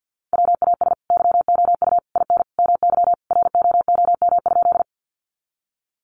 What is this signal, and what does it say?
Morse code, heard as a keyed tone, about 50 wpm, 700 Hz. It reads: WAS YOU IN MY ROOM?
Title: Sentences from Top 300 Words 50wpm